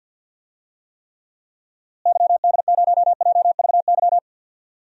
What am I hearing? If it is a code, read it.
Morse code, heard as a keyed tone, about 51 wpm, 700 Hz. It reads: YD0JVY
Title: DXCC-100 Call Signs - 6 Characters - Repeated 2x - 50wpm